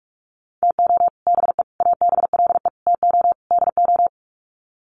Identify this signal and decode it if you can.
Morse code, heard as a keyed tone, about 44 wpm, 700 Hz. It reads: TO BE ABLE TO DO
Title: Sets of 5 Words 45wpm